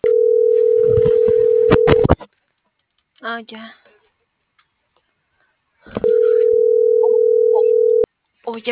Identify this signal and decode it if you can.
Ringback tone (North America), 440 + 480 Hz, with speech.
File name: CENTRAL TERRESTRE / out els..laz.elo.mo.24